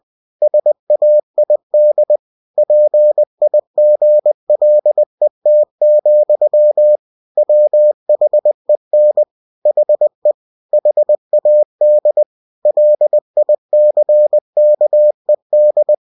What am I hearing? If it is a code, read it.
Morse code, heard as a keyed tone, about 20 wpm, 600 Hz. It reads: SAID PIGLET, WHEN HE HAD LICKED
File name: cw-books / winnie-the-pooh / winnie-20wpm-page-0024